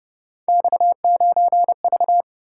Morse code, heard as a keyed tone, about 29 wpm, 700 Hz. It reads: X9V